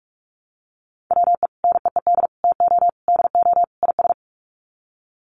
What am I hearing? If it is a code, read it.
Morse code, heard as a keyed tone, about 44 wpm, 700 Hz. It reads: WE NEED TO DO IS
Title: Sets of 5 Words Encore2 45wpm